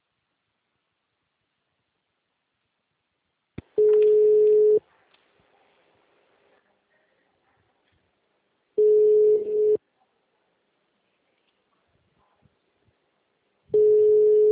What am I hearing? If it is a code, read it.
Ringback tone (Europe, 425 Hz).